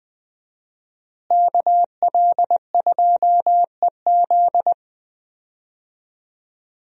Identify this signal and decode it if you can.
Morse code, heard as a keyed tone, about 21 wpm, 700 Hz. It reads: KL2EZ
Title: USA Call Signs - 5 Characters - Repeated 3x - 20wpm